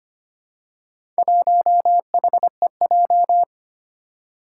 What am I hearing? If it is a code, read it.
Morse code, heard as a keyed tone, about 25 wpm, 700 Hz. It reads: 1HEJ
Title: Four Challenging Letter-Number Combinations 25wpm